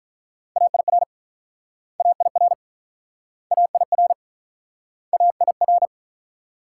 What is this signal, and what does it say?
Morse code, heard as a keyed tone, about 48 wpm, 700 Hz. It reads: AIR AIR AIR AIR